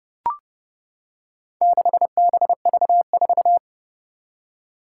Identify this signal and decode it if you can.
Morse code, heard as a keyed tone, about 32 wpm, 700 Hz. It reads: 6BV4
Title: Four Challenging Letter-Number Combinations 30wpm